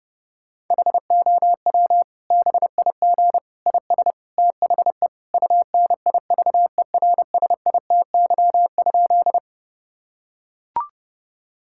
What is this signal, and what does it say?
Morse code, heard as a keyed tone, about 32 wpm, 700 Hz. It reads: HOW BIG IS THE UNIVERSITY?
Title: Sentences from Top 1000 Words - Review Part 5 - 30wpm